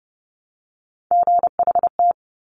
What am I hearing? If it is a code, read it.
Morse code, heard as a keyed tone, about 30 wpm, 690 Hz. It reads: GHT
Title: Top Three Letter Combinations in Words 30wpm